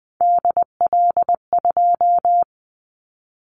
Morse code, heard as a keyed tone, about 20 wpm, 700 Hz. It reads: DL2
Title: Callsign Prefixes for Top 25 DXCC - 20wpm